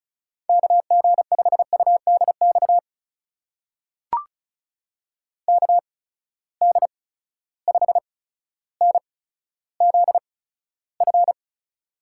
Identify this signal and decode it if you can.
Morse code, heard as a keyed tone, about 36 wpm, spaced as for 15 wpm, 700 Hz. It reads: KG5UDX KD5NZF
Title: USA Call Signs - 6 Characters - 8x Character Spacing - 35wpm